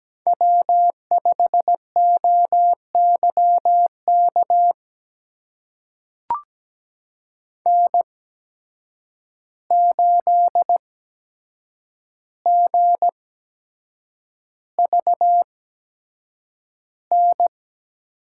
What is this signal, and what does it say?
Morse code, heard as a keyed tone, about 17 wpm, spaced as for 7 wpm, 700 Hz. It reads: W5OYK N8GVN
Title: USA Call Signs - 5 Characters - 8x Character Spacing - 17wpm